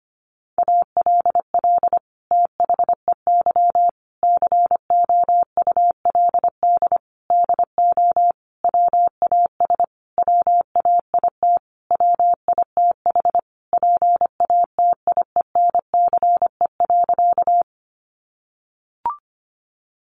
Morse code, heard as a keyed tone, about 25 wpm, 700 Hz. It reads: ALL THEY COULD DO WAS WAIT WITH PATIENCE.
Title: Sentences from Top 1300 Words 25wpm